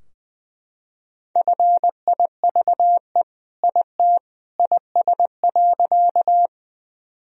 Morse code, heard as a keyed tone, about 19 wpm, 700 Hz. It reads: FIVE IT IS.
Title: Sentences from Top 1000 Words - Review Part 2 - 20wpm